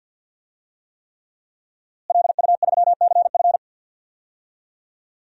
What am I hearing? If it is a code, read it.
Morse code, heard as a keyed tone, about 51 wpm, 700 Hz. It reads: LU3CF
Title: DXCC-100 Call Signs - 5 Characters - Repeated 3x - 50wpm